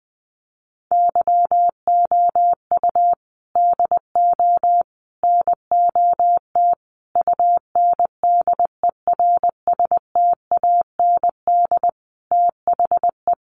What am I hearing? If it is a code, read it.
Morse code, heard as a keyed tone, about 20 wpm, 700 Hz. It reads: YOU DO NOT UNDERSTAND THE
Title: Sentences from Top 600 Words 20wpm